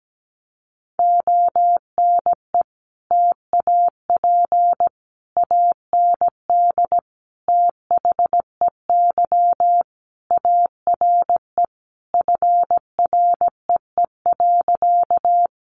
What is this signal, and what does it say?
Morse code, heard as a keyed tone, about 17 wpm, 700 Hz. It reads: ONE TAP AND THEY ARE FREE.
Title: Sentences from Top 1400 Words 17wpm